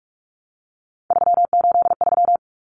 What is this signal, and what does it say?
Morse code, heard as a keyed tone, about 44 wpm, 700 Hz. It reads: 383
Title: Numbers - 3-Digit 45wpm